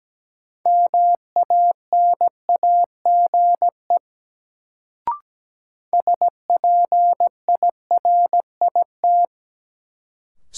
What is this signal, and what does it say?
Morse code, heard as a keyed tone, about 17 wpm, 700 Hz. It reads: MANAGE SPIRIT
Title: Sentences from Top 1400 Words - New Word Drill Alt - 17wpm